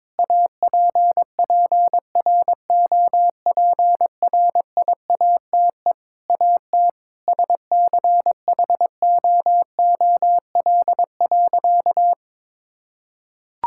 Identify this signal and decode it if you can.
Morse code, heard as a keyed tone, about 21 wpm, 700 Hz. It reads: APPROPRIATE AT SCHOOL.
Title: Sentences from Top 1000 Words - Review Part 5 - 22wpm